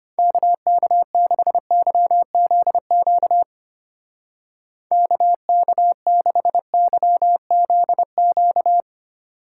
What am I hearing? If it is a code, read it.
Morse code, heard as a keyed tone, about 32 wpm, 700 Hz. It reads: KK6YZQ KK6YZQ